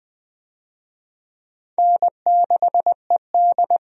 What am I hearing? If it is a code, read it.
Morse code, heard as a keyed tone, about 20 wpm, 700 Hz. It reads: N6ED
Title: USA Call Signs - 4 Characters - 20wpm